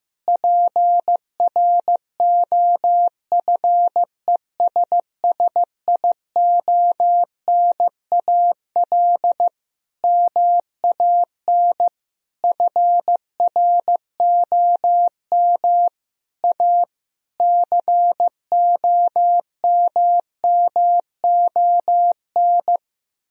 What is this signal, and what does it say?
Morse code, heard as a keyed tone, about 16 wpm, 700 Hz. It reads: PROFESSIONAL MAN FROM A COMMON
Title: Sentences from Top 1000 Words - Review Part 4 - 15wpm